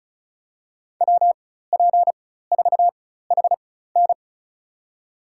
Morse code, heard as a keyed tone, about 36 wpm, spaced as for 16 wpm, 700 Hz. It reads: WP4HN
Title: USA Call Signs - 5 Characters - 4x Character Spacing - 35wpm